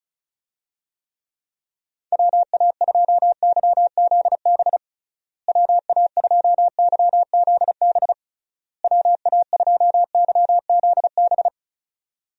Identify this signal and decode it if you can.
Morse code, heard as a keyed tone, about 36 wpm, 700 Hz. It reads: WA2YZB WA2YZB WA2YZB
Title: USA Call Signs - 6 Characters - Repeated 3x - 35wpm